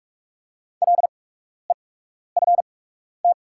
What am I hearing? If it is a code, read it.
Morse code, heard as a keyed tone, about 46 wpm, spaced as for 12 wpm, 700 Hz. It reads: LEFT
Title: 8x Character Spacing - Top 500 Words - 45wpm